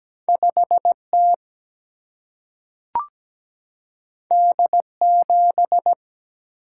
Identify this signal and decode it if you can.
Morse code, heard as a keyed tone, about 17 wpm, 700 Hz. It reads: 5 T D 7
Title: Single-Letter Single-Number 17wpm